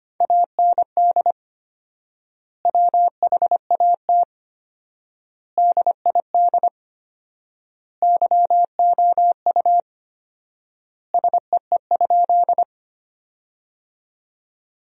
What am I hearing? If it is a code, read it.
Morse code, heard as a keyed tone, about 25 wpm, 700 Hz. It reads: AND WHAT DID YOU SEE?
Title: 4x Word-Spacing - Sentences from Top 100 Words - 25wpm